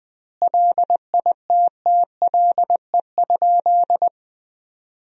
Morse code, heard as a keyed tone, about 20 wpm, 700 Hz. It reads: LITTLE?
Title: 2x Word-Spacing - Sentences from Top 500 Words - 20wpm